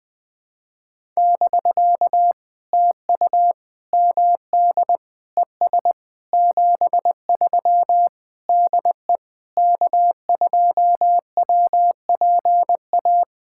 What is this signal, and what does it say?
Morse code, heard as a keyed tone, about 20 wpm, 700 Hz. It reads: <BK> TU MD ES 73 DE K2WPA
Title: POTA QSO Practice - 20wpm